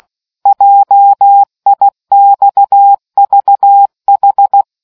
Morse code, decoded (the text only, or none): JIXVH